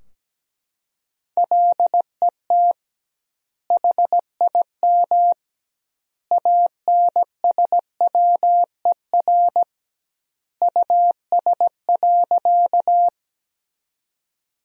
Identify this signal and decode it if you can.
Morse code, heard as a keyed tone, about 17 wpm, 700 Hz. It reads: LET HIM ANSWER US.
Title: 2x Word-Spacing - Sentences from Top 300 Words - 17wpm